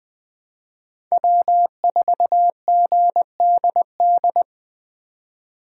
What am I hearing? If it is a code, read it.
Morse code, heard as a keyed tone, about 20 wpm, 700 Hz. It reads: W4GDD
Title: USA Call Signs - 4x Character Spacing - 20wpm